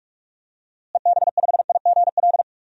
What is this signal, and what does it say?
Morse code, heard as a keyed tone, about 41 wpm, 700 Hz. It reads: EB5IGL
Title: DXCC-100 Call Signs - 6 Characters - Repeated 3x - 45wpm